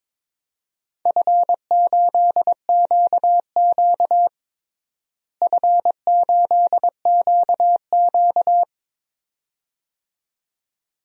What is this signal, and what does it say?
Morse code, heard as a keyed tone, about 22 wpm, 700 Hz. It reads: F8QQ F8QQ
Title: DXCC-100 Call Signs - 4 Characters - Repeated 2x - 22wpm